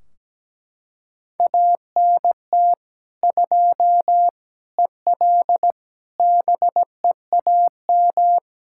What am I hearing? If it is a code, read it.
Morse code, heard as a keyed tone, about 17 wpm, 700 Hz. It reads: ANT 2 EL BEAM
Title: Practice QSO - Exchange 2 - Info - 17wpm